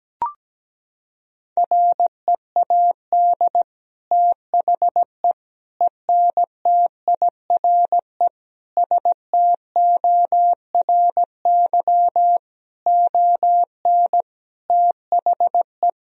Morse code, heard as a keyed tone, about 17 wpm, 700 Hz. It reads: READ THE ENTIRE STORY ON THE
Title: Sentences from Top 1000 Words - Review Part 4 - 17wpm